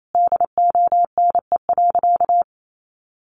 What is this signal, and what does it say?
Morse code, heard as a keyed tone, about 28 wpm, 700 Hz. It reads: DONE.
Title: Sentences from Top 500 Words 28wpm